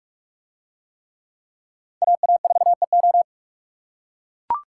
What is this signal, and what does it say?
Morse code, heard as a keyed tone, about 44 wpm, 700 Hz. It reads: AA4EO